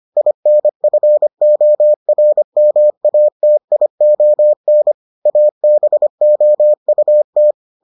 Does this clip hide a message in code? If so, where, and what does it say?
Yes: Morse code from 0.2 to 7.5 s, heard as a keyed tone, about 25 wpm, 600 Hz. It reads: INFORMATION ABOUT